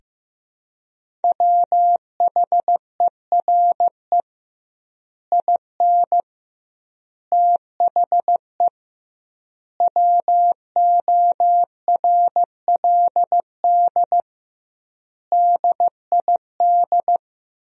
Morse code, heard as a keyed tone, about 15 wpm, 700 Hz. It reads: WHERE IN THE WORLD DID
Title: 2x Word-Spacing - Sentences from Top 200 Words - 15wpm